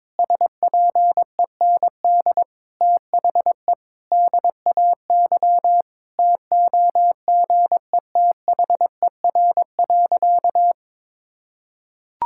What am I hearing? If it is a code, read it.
Morse code, heard as a keyed tone, about 22 wpm, 700 Hz. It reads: SPEND THE DAY TOGETHER.
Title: Sentences from Top 1000 Words - Review Part 5 - 22wpm